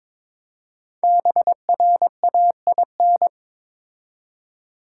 Morse code, heard as a keyed tone, about 22 wpm, 700 Hz. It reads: BRAIN
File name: 5 Letter Words - First 3 Letters Given - 22wpm